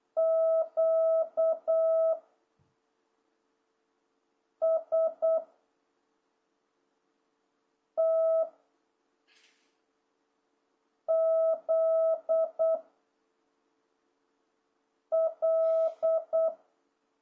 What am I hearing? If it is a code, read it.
Morse code, heard as a keyed tone, about 8 wpm, 660 Hz. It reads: Q S T Z L